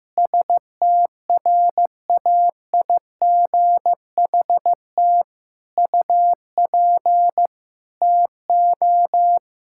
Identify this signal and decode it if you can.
Morse code, heard as a keyed tone, about 15 wpm, 700 Hz. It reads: STRAIGHT UP TO
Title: Sentences from Top 1000 Words - Review Part 6 - 15wpm